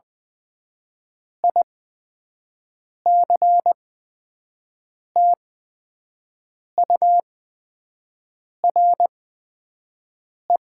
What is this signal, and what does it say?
Morse code, heard as a keyed tone, about 20 wpm, spaced as for 5 wpm, 700 Hz. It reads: ICTURE
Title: 8x Character Spacing - Top 300 Words - 20wpm